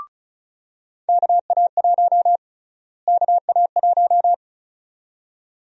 Morse code, heard as a keyed tone, about 36 wpm, 700 Hz. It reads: KA1 KA1
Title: USA Call Sign Prefixes - Repeated 2x - 35wpm